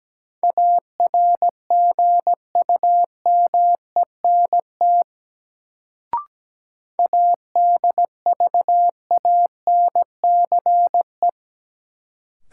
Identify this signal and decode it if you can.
Morse code, heard as a keyed tone, about 17 wpm, 700 Hz. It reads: ARGUMENT ADVANCE
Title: Sentences from Top 1400 Words - New Word Drill - 17wpm